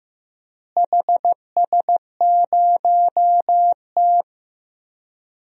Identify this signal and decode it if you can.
Morse code, heard as a keyed tone, about 15 wpm, 700 Hz. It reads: HS0T
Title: DXCC-100 Call Signs - 4 Characters - Repeated 3x - 15wpm